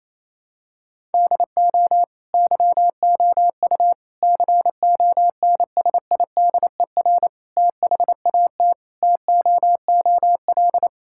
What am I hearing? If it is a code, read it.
Morse code, heard as a keyed tone, about 28 wpm, 700 Hz. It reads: DO YOU CONSIDER THAT TOOL
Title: Sentences from Top 1000 Words - Review Part 6 - 28wpm